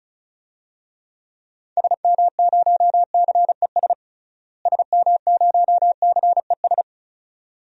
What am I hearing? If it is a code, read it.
Morse code, heard as a keyed tone, about 34 wpm, 700 Hz. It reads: SM0CES SM0CES
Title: DXCC-100 Call Signs - 6 Characters - Repeated 2x - 35wpm